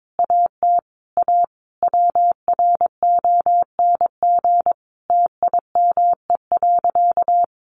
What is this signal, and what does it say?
Morse code, heard as a keyed tone, about 22 wpm, 700 Hz. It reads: AT A WRONG TIME.
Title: Sentences from Top 900 Words 22wpm